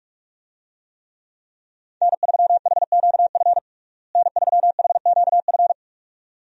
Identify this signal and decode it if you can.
Morse code, heard as a keyed tone, about 46 wpm, 700 Hz. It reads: N3HQF N3HQF
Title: USA Call Signs - 5 Characters - Repeated 2x - 45wpm